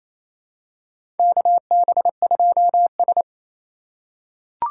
Morse code, heard as a keyed tone, about 28 wpm, 700 Hz. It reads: KB2S
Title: K1USN SST QSO Practice - 28wpm